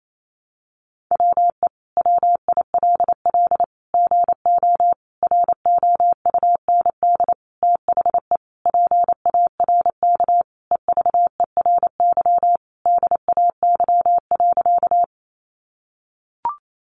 Morse code, heard as a keyed tone, about 28 wpm, 700 Hz. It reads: WE WILL GO ROUND THE PARK EVERY DAY.
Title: Sentences from Top 1300 Words 28wpm